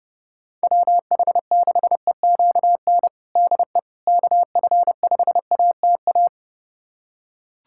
Morse code, heard as a keyed tone, about 30 wpm, 700 Hz. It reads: WH6EQN DE KF5ATA
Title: USA Callsign DE Callsign - 6 Characters - 30wpm